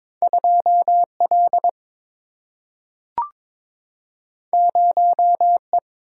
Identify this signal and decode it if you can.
Morse code, heard as a keyed tone, about 22 wpm, 700 Hz. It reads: 2 L 0 E
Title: Single-Letter Single-Number 22wpm